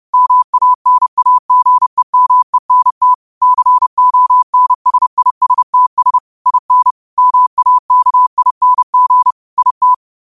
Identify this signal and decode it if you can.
Morse code, heard as a keyed tone, about 30 wpm, 1000 Hz. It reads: MANAGEMENT CONSISTS IN MAKING IT
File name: qotdfiles / 30wpm